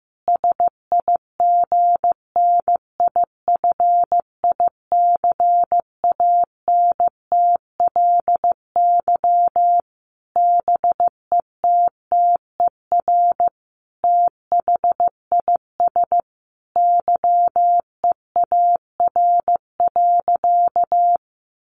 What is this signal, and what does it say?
Morse code, heard as a keyed tone, about 15 wpm, 700 Hz. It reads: SIGNIFICANTLY BETTER THIS YEAR.
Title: Sentences from Top 1500 Words 15wpm